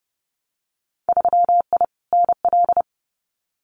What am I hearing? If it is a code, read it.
Morse code, heard as a keyed tone, about 30 wpm, 700 Hz. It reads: 3I NL
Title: Winter Field Day Exchanges 30wpm